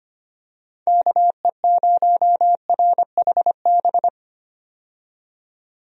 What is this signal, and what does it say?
Morse code, heard as a keyed tone, about 25 wpm, 700 Hz. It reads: KE0RHB
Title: USA Call Signs - 6 Characters - 25wpm